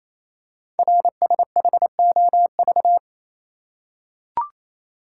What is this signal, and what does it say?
Morse code, heard as a keyed tone, about 28 wpm, 700 Hz. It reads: RSHOV
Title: Code Groups - 5 Letter:Numbers - 28wpm